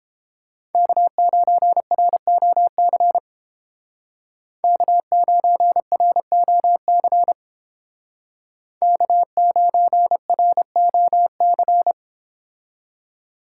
Morse code, heard as a keyed tone, about 29 wpm, 700 Hz. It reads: K9ROC K9ROC K9ROC